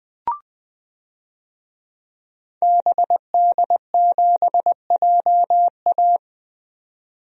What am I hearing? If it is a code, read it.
Morse code, heard as a keyed tone, about 20 wpm, 700 Hz. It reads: BD7JA
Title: DXCC-100 Call Signs - 5 Characters - 20wpm